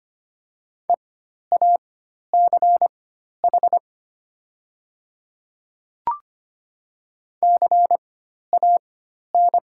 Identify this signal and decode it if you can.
Morse code, heard as a keyed tone, about 25 wpm, spaced as for 12 wpm, 700 Hz. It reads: EACH CAN
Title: 4x Character Spacing - Top 100 Words - 25wpm